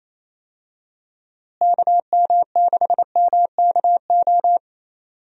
Morse code, heard as a keyed tone, about 28 wpm, 700 Hz. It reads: KM6MKO
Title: USA Call Signs - 6 Characters - Repeated 3x - 28wpm